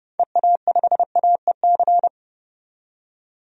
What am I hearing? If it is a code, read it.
Morse code, heard as a keyed tone, about 30 wpm, 700 Hz. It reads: EA5AEC
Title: DXCC-100 Call Signs - 6 Characters - 30wpm